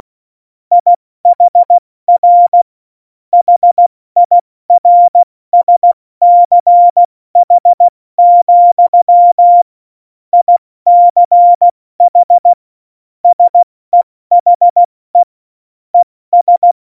Morse code, heard as a keyed tone, about 16 wpm, spaced as for 14 wpm, 700 Hz. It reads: IHR HIRSCH, ICH SEHE ES
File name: Morsekurs / Witze / Witz 13. Oktober 2020, 70 F80